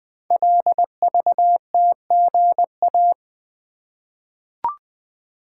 Morse code, heard as a keyed tone, about 20 wpm, 700 Hz. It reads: LVTGA
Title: Code Groups - 5 Letters - 20wpm